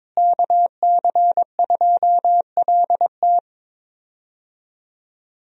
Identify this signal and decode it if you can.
Morse code, heard as a keyed tone, about 22 wpm, 700 Hz. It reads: KC2LT
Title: USA Call Signs - 5 Characters - Repeated 2x - 22wpm